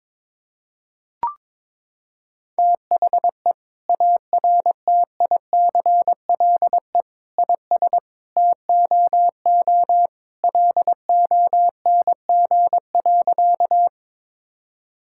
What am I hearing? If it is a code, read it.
Morse code, heard as a keyed tone, about 22 wpm, 700 Hz. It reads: THE ARTICLE IS TOO LONG.
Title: Sentences from Top 1000 Words - Review Part 5 - 22wpm